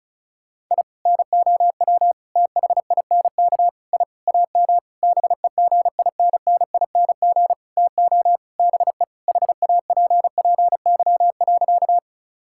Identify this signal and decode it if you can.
Morse code, heard as a keyed tone, about 34 wpm, 700 Hz. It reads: I NOW THINK I AM BEGINNING TO BE HAPPY.
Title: Sentences from Top 1000 Words - Review Part 5 - 35wpm